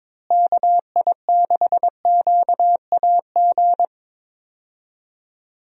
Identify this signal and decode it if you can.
Morse code, heard as a keyed tone, about 22 wpm, 700 Hz. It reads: KI6QAG
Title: USA Call Signs - 6 Characters - Repeated 3x - 22wpm